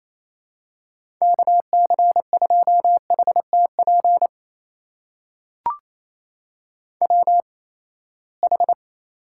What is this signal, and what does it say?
Morse code, heard as a keyed tone, about 28 wpm, spaced as for 19 wpm, 700 Hz. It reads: KC2HTP WH
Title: USA Call Signs - 6 Characters - 8x Character Spacing - 28wpm